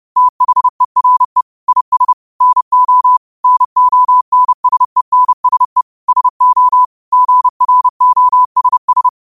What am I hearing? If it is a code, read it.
Morse code, heard as a keyed tone, about 30 wpm, 1000 Hz. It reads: THERE IS NO NONSENSE SO GROSS